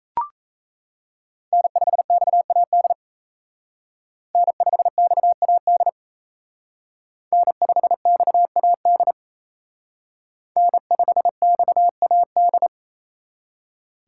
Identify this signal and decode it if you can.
Morse code, heard as a keyed tone, about 32 wpm, 700 Hz. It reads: N5XAD N5XAD N5XAD N5XAD
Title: Speed Racing - USA 1x3 Call Signs 28wpm